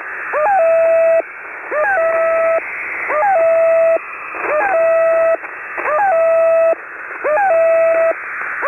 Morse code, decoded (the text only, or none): none